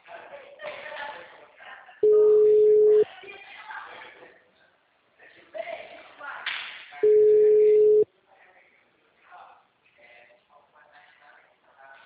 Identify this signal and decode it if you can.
Ringback tone (Europe, 425 Hz), with speech.
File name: out- -2007- - - .20369